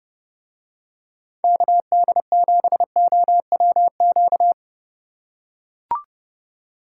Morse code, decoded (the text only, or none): KD7OWQ